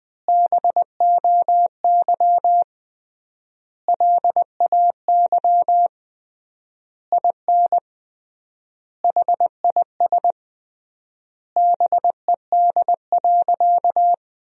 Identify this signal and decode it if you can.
Morse code, heard as a keyed tone, about 20 wpm, 700 Hz. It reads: BOY LAY IN HIS BED.
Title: 3x Word-Spacing - Sentences from Top 500 Words - 20wpm